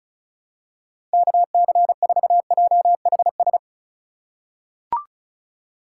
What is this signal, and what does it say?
Morse code, heard as a keyed tone, about 36 wpm, 700 Hz. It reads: KC4JHS
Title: USA Call Signs - 6 Characters - 3x Character Spacing - 35wpm